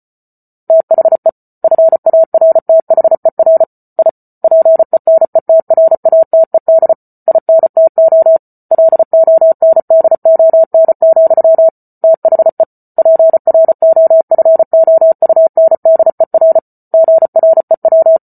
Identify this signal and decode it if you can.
Morse code, heard as a keyed tone, about 34 wpm, 650 Hz. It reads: THE FARTHER I PENETRATED INTO LONDON, THE PROFOUNDER GREW